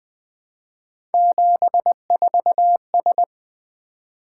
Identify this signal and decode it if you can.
Morse code, heard as a keyed tone, about 20 wpm, 700 Hz. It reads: 74S
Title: Three Letter-Number Combinations - 20wpm